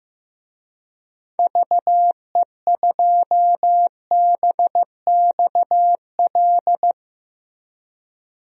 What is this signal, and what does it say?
Morse code, heard as a keyed tone, about 15 wpm, 700 Hz. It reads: VE2BXL
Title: DXCC-100 Call Signs - 6 Characters - Repeated 2x - 15wpm